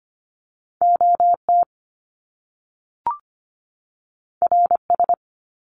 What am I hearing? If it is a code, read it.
Morse code, heard as a keyed tone, about 25 wpm, 700 Hz. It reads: OT RS
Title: Top Two Letter Combinations in Words 25wpm